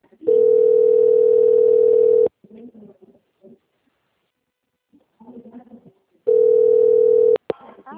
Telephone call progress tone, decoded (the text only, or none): ringback tone